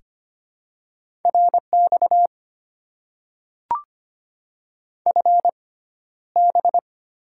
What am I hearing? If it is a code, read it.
Morse code, heard as a keyed tone, about 25 wpm, spaced as for 13 wpm, 700 Hz. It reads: RX FB
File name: Top 90 QSO Elements - 6x Character Spacing - 25wpm